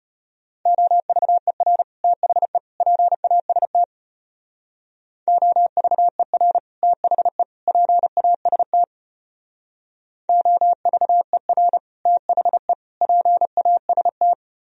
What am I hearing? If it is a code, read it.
Morse code, heard as a keyed tone, about 31 wpm, 700 Hz. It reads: OVER THE PAST OVER THE PAST OVER THE PAST
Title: Speed-Racing - Sets of 3 Words 25wpm